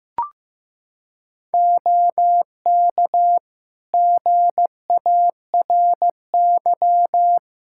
Morse code, heard as a keyed tone, about 16 wpm, 700 Hz. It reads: OK GARY
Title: Practice QSO - Exchange 4 - Ending - 15wpm